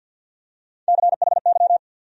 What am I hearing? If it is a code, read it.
Morse code, heard as a keyed tone, about 48 wpm, 700 Hz. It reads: CHY
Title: USA Call Sign Postfixes - 3 Letters 50wpm